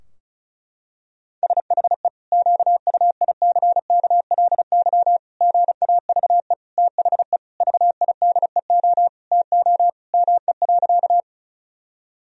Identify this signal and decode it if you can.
Morse code, heard as a keyed tone, about 36 wpm, 700 Hz. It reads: SHE QUICKLY GAVE THE VIDEO TO ME.
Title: Sentences from Top 1000 Words - Review Part 4 - 35wpm